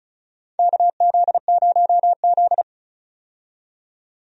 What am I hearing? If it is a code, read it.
Morse code, heard as a keyed tone, about 36 wpm, 700 Hz. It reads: KZ0Z
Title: USA Call Signs - 4 Characters - 35wpm